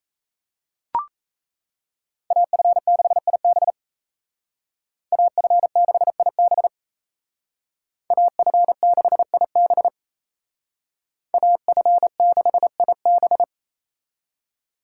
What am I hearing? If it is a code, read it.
Morse code, heard as a keyed tone, about 36 wpm, 700 Hz. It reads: AF6IB AF6IB AF6IB AF6IB